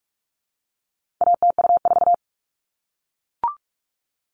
Morse code, heard as a keyed tone, about 44 wpm, 700 Hz. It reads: ATU4
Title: All Cut Numbers - 4-Digit 45wpm